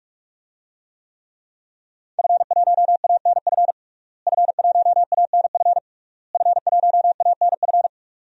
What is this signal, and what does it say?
Morse code, heard as a keyed tone, about 46 wpm, 700 Hz. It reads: F1ANF F1ANF F1ANF
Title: DXCC-100 Call Signs - 5 Characters - Repeated 3x - 45wpm